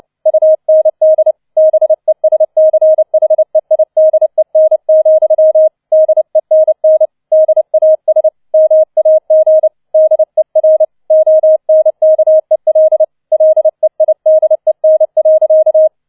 Morse code, heard as a keyed tone, about 29 wpm, 630 Hz. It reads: UND BESCHEIDEN, DENN DAS MAG DER ONKEL LEIDEN.